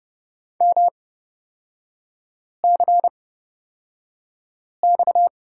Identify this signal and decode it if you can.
Morse code, heard as a keyed tone, about 30 wpm, spaced as for 5 wpm, 700 Hz. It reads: MCX
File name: Lesson 37e - Letter x - Three Characters 30wpm